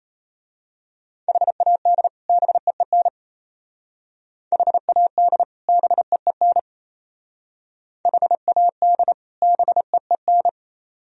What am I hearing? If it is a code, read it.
Morse code, heard as a keyed tone, about 31 wpm, 700 Hz. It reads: HAD BEEN HAD BEEN HAD BEEN
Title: Speed-Racing - Sets of 2 Words 28wpm